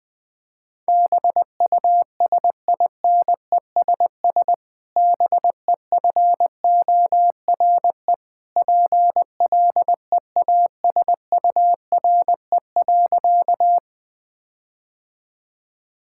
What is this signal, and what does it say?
Morse code, heard as a keyed tone, about 20 wpm, 700 Hz. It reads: BUSINESS BEFORE PLEASURE.
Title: Sentences from Top 1600 Words 20wpm